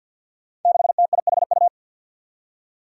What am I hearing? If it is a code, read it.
Morse code, heard as a keyed tone, about 50 wpm, 700 Hz. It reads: 6TIHU
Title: Five Challenging Letter-Number Combinations 50wpm